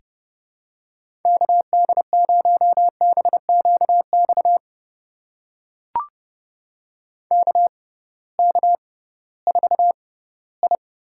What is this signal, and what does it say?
Morse code, heard as a keyed tone, about 30 wpm, spaced as for 17 wpm, 700 Hz. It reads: KD0BQX KK4I
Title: USA Call Signs - 6 Characters - 6x Character Spacing - 30wpm